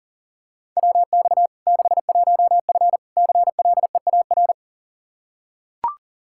Morse code, heard as a keyed tone, about 40 wpm, 700 Hz. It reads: WX 61F CLEAR
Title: K1USN SST QSO Practice 40wpm